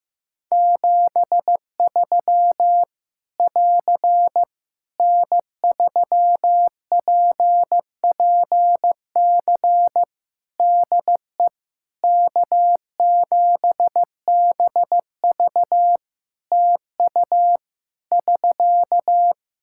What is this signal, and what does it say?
Morse code, heard as a keyed tone, about 15 wpm, 700 Hz. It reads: 73 <AR> N3PPC DE K7BV TU <SK>
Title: Practice QSO - Exchange 4 - Ending - 15wpm